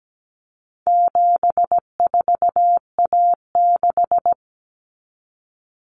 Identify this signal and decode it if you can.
Morse code, heard as a keyed tone, about 17 wpm, 700 Hz. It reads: 74A6
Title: Common Cut Numbers - 4-Digit 17wpm